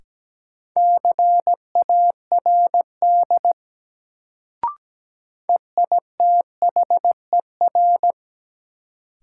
Morse code, heard as a keyed tone, about 17 wpm, 700 Hz. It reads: CARD EITHER
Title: Sentences from Top 600 Words - New Word Drill - 17wpm